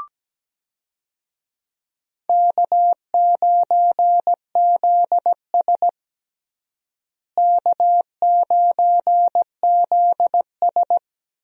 Morse code, heard as a keyed tone, about 17 wpm, 700 Hz. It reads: K9ZS K9ZS
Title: USA Call Signs - 4 Characters - Repeated 3x - 17wpm